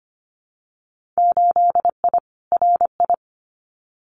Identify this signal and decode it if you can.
Morse code, heard as a keyed tone, about 25 wpm, 700 Hz. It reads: 8I RI
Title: Winter Field Day Exchanges Encore 25wpm